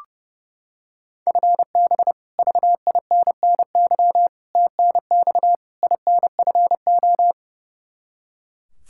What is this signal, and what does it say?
Morse code, heard as a keyed tone, about 30 wpm, 700 Hz. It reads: FB VINNY TNX INFO
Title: Practice QSO - Exchange 3 - More Info - 30wpm